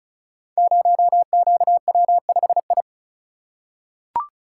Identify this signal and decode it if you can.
Morse code, heard as a keyed tone, about 36 wpm, 700 Hz. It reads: 0QW5I
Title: Code Groups - 5 Letter:Numbers - 35wpm